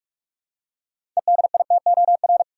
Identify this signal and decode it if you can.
Morse code, heard as a keyed tone, about 42 wpm, 700 Hz. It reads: EDITOR
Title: Six Letter Words - 45wpm